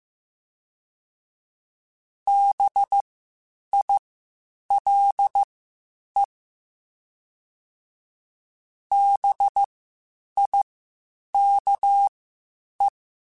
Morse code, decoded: BILE BIKE

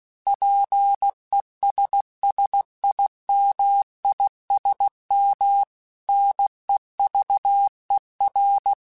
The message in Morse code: PESSIMISM NEVER

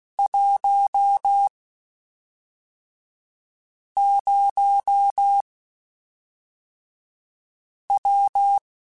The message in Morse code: 10W

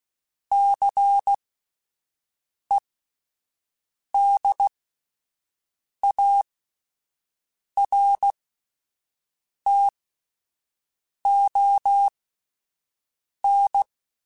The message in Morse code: CEDARTON

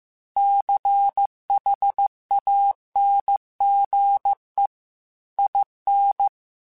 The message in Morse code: CHANGE IN

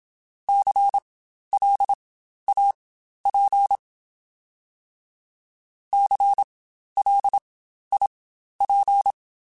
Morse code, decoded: CLAP CLIP

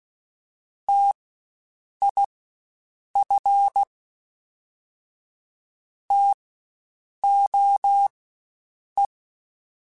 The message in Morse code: TIF TOE

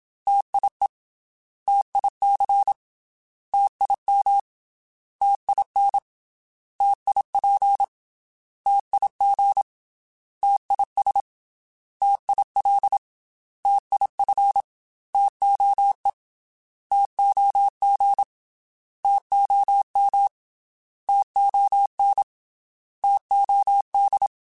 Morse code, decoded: TIE TIC TIM TIN TIP TIG TIS TIL TIF TOE TOG TOM TON TOD

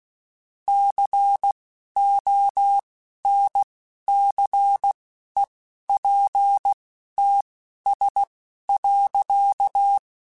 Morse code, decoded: CONCEPTS.

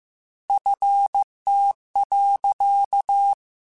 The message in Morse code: FT.

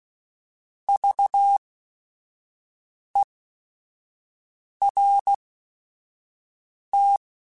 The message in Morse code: VERT

